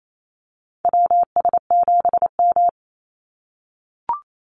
Morse code, heard as a keyed tone, about 28 wpm, 700 Hz. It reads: WS7M
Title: Call Signs - 28wpm